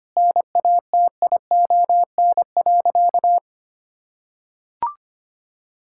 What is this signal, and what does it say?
Morse code, heard as a keyed tone, about 25 wpm, 700 Hz. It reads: NATION.
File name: practice / 25wpm / Difficult Sentences with Hints - 25wpm